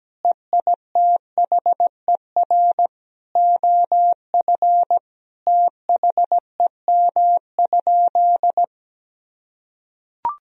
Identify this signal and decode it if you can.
Morse code, heard as a keyed tone, about 17 wpm, 700 Hz. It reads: EITHER OF THEM?
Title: Sentences from Top 1000 Words - Review Part 4 - 17wpm